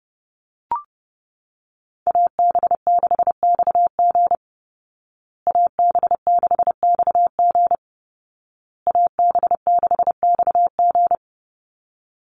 Morse code, heard as a keyed tone, about 30 wpm, 700 Hz. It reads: AB6XG AB6XG AB6XG
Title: Call Signs Repeated 2x - 30wpm